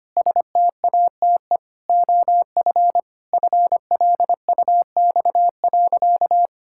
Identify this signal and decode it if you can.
Morse code, heard as a keyed tone, about 25 wpm, 700 Hz. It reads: STATE OF FLUX.
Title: Pangram Challenge v2 - 25wpm